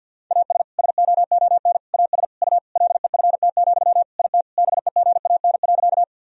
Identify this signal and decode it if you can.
Morse code, heard as a keyed tone, about 52 wpm, 700 Hz. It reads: AS SOON AS U LEFT, IT BEGAN.
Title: Sentences from Top 400 Words and Abbreviations 50wpm